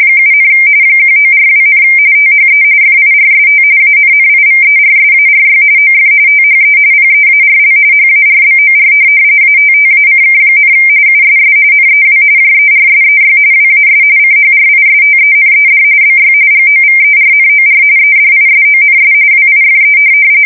RTTY, about 45.45 baud. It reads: 7, 1942, THE FIRST ANNIVERSARY OF THE PEARL HARBOR ATTACK, USS NEW JERSEY WAS CHRISTENED BY GOVERNOR EDISONS WIFE
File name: fsk170 / USS-New-Jersey